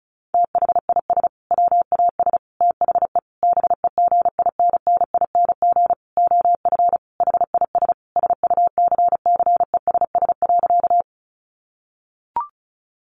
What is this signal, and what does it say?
Morse code, heard as a keyed tone, about 34 wpm, 700 Hz. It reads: THIS WAS THE BEGINNING OF HIS SUCCESS.
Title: Sentences from Top 900 Words 35wpm